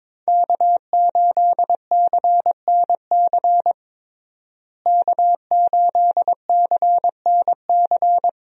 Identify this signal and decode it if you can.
Morse code, heard as a keyed tone, about 22 wpm, 700 Hz. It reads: K8CNC K8CNC
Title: USA Call Signs - 5 Characters - Repeated 3x - 22wpm